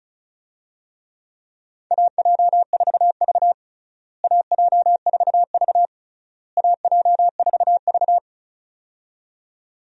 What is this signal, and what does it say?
Morse code, heard as a keyed tone, about 34 wpm, 700 Hz. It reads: AJ4V AJ4V AJ4V
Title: USA Call Signs - 4 Characters - Repeated 3x - 35wpm